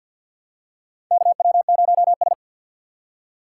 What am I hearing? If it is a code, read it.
Morse code, heard as a keyed tone, about 54 wpm, 700 Hz. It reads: KW0S
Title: USA Call Signs - 4 Characters - 3x Character Spacing - 50wpm